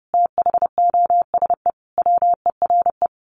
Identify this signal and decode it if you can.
Morse code, heard as a keyed tone, about 30 wpm, 700 Hz. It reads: THOSE WERE
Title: A Sound Of Thunder 30wpm